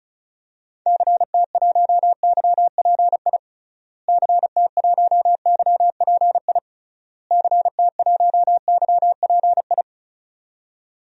Morse code, heard as a keyed tone, about 36 wpm, 700 Hz. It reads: CT1YPI CT1YPI CT1YPI
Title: DXCC-100 Call Signs - 6 Characters - Repeated 3x - 35wpm